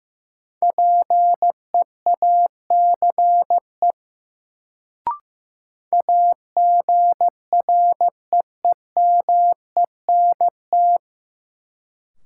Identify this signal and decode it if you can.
Morse code, heard as a keyed tone, about 16 wpm, 700 Hz. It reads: PEACE AGREEMENT
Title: Sentences from Top 1500 Words - New Word Drill - 15wpm